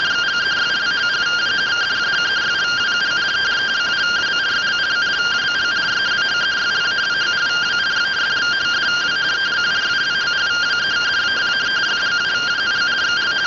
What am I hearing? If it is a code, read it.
Two-tone teleprinter signal, about 170 Hz shift.